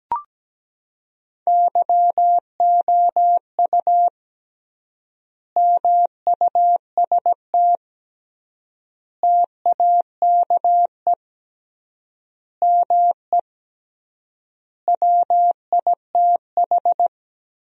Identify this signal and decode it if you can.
Morse code, heard as a keyed tone, about 17 wpm, 700 Hz. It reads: YOU MUST TAKE ME WITH
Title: 3x Word-Spacing - Sentences from Top 200 Words - 17wpm